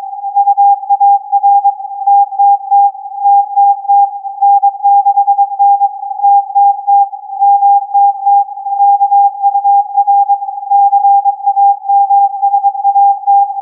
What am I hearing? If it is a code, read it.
Morse code, heard as a keyed tone, about 23 wpm, 800 Hz. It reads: UAR TTT TTT N6N TTT MTT KUR CAMSAT
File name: CAS4B-TURN-AIRSPY-01